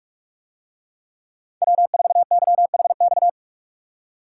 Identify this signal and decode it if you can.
Morse code, heard as a keyed tone, about 40 wpm, 700 Hz. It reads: W4YHX